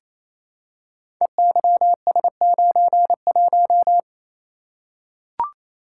Morse code, heard as a keyed tone, about 28 wpm, 700 Hz. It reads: EYS91
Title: Code Groups - 5 Letter:Numbers - 28wpm